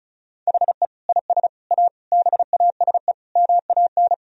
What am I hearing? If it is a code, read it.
Morse code, heard as a keyed tone, about 36 wpm, 700 Hz. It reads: HE IS A BASE MAN
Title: Sentences from Top 1000 Words - Review Part 1 - 35wpm